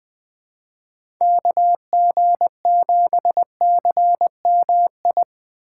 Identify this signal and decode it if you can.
Morse code, heard as a keyed tone, about 20 wpm, 700 Hz. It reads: KG7CMI
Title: USA Call Signs - 3x Character Spacing - 20wpm